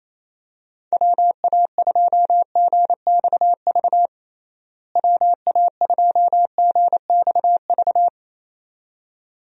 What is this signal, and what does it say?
Morse code, heard as a keyed tone, about 27 wpm, 700 Hz. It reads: WA2GXV WA2GXV
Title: USA Call Signs - 6 Characters - Repeated 2x - 28wpm